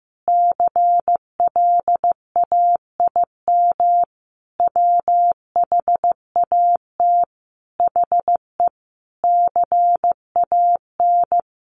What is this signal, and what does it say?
Morse code, heard as a keyed tone, about 15 wpm, 700 Hz. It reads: CLAIM WHAT HE CAN
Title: Sentences from Top 1500 Words 15wpm